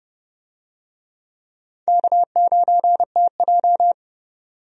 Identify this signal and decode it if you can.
Morse code, heard as a keyed tone, about 30 wpm, 700 Hz. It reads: K9TJ
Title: USA Call Signs - 4 Characters - Repeated 2x - 30wpm